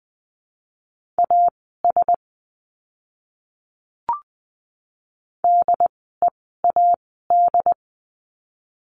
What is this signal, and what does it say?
Morse code, heard as a keyed tone, about 20 wpm, spaced as for 15 wpm, 700 Hz. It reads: AS DEAD
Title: 075 Lesson 13f - Letter L - Words 20wpm